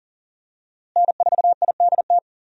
Morse code, heard as a keyed tone, about 41 wpm, 700 Hz. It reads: N4IDT
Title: USA Call Signs - 5 Characters - Repeated 3x - 40wpm